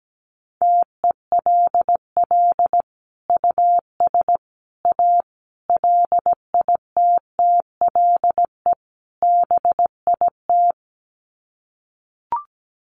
Morse code, heard as a keyed tone, about 17 wpm, 700 Hz. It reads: TELL US A LITTLE BIT
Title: Sets of 5 Words Encore2 17wpm